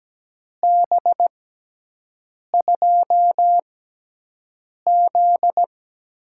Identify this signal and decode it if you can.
Morse code, heard as a keyed tone, about 17 wpm, spaced as for 6 wpm, 700 Hz. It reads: B2Z
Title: USA Call Signs - 6x Character Spacing - 17wpm